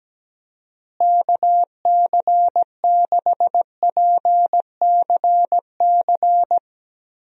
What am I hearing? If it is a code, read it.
Morse code, heard as a keyed tone, about 17 wpm, 700 Hz. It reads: KC6PCC
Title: USA Call Signs - 6 Characters - Repeated 3x - 17wpm